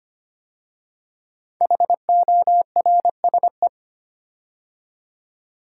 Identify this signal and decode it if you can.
Morse code, heard as a keyed tone, about 25 wpm, 700 Hz. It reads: HORSE
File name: 5 Letter Words - First 3 Letters Given - 25wpm